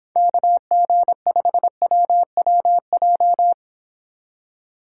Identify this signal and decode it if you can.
Morse code, heard as a keyed tone, about 26 wpm, 700 Hz. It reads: KG5WWJ